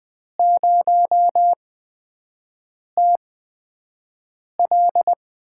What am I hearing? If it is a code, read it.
Morse code, heard as a keyed tone, about 20 wpm, spaced as for 5 wpm, 700 Hz. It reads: 0TL
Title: USA Call Signs - 8x Character Spacing - 20wpm